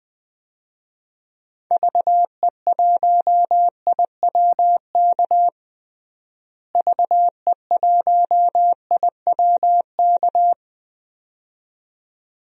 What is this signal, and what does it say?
Morse code, heard as a keyed tone, about 21 wpm, 700 Hz. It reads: VE1IWK VE1IWK
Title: DXCC-100 Call Signs - 6 Characters - Repeated 2x - 20wpm